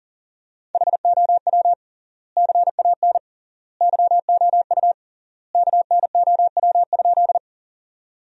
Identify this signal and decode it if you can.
Morse code, heard as a keyed tone, about 40 wpm, 700 Hz. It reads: HOW CAN YOU KNOW?
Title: 3x Word-Spacing - Sentences from Top 100 Words - 40wpm